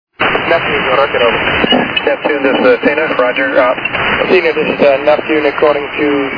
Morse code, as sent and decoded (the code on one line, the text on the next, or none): none